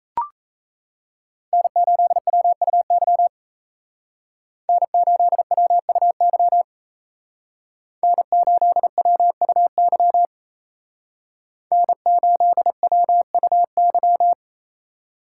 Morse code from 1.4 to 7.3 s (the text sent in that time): N8WUY N8WUY